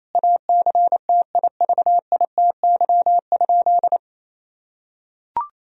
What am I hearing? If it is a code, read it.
Morse code, heard as a keyed tone, about 28 wpm, 700 Hz. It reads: ACTIVITY?
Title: Sentences from Top 1000 Words - Review Part 4 - 28wpm